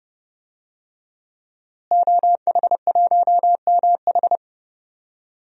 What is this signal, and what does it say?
Morse code, heard as a keyed tone, about 32 wpm, 700 Hz. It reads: OH1MH
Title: DXCC-100 Call Signs - 5 Characters - Repeated 2x - 30wpm